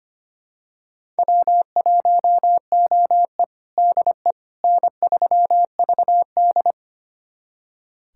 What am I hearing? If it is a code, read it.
Morse code, heard as a keyed tone, about 25 wpm, 700 Hz. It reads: W1OE DE N3VD
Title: USA Callsign DE Callsign - 4 Characters - 25wpm